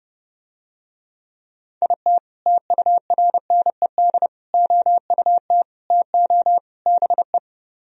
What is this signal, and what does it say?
Morse code, heard as a keyed tone, about 30 wpm, 700 Hz. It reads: IT TURNED OUT TO BE